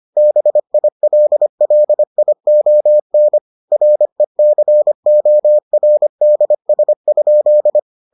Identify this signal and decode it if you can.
Morse code, heard as a keyed tone, about 25 wpm, 600 Hz. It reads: BILLION RECORDS?